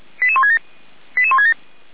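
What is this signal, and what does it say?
Five-tone selective call (ZVEI), heard as quick runs of five tones, tones of about 80 ms. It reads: 88167 88167
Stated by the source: ZVEI five-tone selective call received on an SDR, recorded by feuerwehr-orlamuende.org (Feuerwehr Orlamünde).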